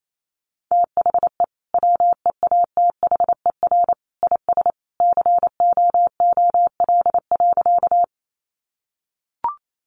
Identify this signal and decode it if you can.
Morse code, heard as a keyed tone, about 28 wpm, 700 Hz. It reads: THE WEATHER IS COOL.
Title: Sentences from Top 1200 Words 28wpm